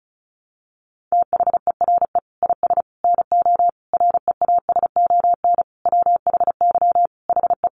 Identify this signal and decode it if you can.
Morse code, heard as a keyed tone, about 34 wpm, 700 Hz. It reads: THERE IS NO REASON WHY HE
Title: Sentences from Top 700 Words 35wpm